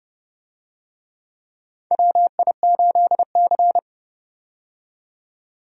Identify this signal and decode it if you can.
Morse code, heard as a keyed tone, about 30 wpm, 700 Hz. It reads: WI8C
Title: USA Call Signs - 4 Characters - 30wpm